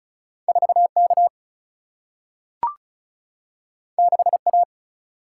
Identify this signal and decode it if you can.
Morse code, heard as a keyed tone, about 36 wpm, 700 Hz. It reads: 4K 6A
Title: Single-Letter Single-Number 35wpm